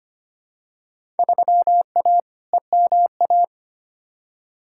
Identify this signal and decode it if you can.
Morse code, heard as a keyed tone, about 25 wpm, 700 Hz. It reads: 3A EMA
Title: Field Day Exchanges - 25wpm